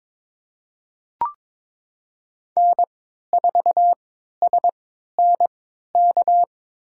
Morse code, heard as a keyed tone, about 22 wpm, spaced as for 13 wpm, 700 Hz. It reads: N4SNK
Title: USA Call Signs - 5 Characters - 3x Character Spacing - 22wpm